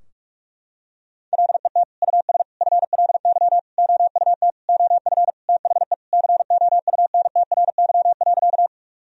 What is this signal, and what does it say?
Morse code, heard as a keyed tone, about 46 wpm, 700 Hz. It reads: LET US FLY OUT OF THE COUNTRY.
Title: Sentences from Top 1000 Words - Review Part 2 - 45wpm